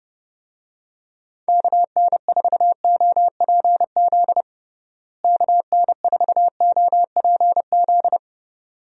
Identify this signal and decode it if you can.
Morse code, heard as a keyed tone, about 30 wpm, 700 Hz. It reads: KN4OPZ KN4OPZ
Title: USA Call Signs - 6 Characters - Repeated 2x - 30wpm